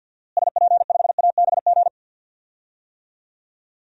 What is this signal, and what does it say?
Morse code, heard as a keyed tone, about 52 wpm, 700 Hz. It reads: SP5ABG
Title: DXCC-100 Call Signs - 6 Characters - Repeated 2x - 50wpm